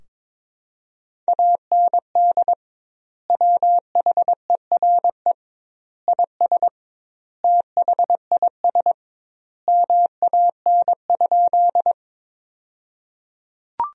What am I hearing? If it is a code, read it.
Morse code, heard as a keyed tone, about 22 wpm, 700 Hz. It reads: AND WHERE IS THIS MAN?
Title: 2x Word-Spacing - Sentences from Top 200 Words - 22wpm